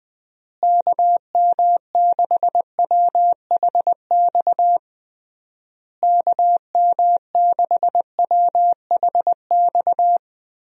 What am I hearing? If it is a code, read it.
Morse code, heard as a keyed tone, about 20 wpm, 700 Hz. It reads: KM6WHX KM6WHX
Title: USA Call Signs - 6 Characters - Repeated 3x - 20wpm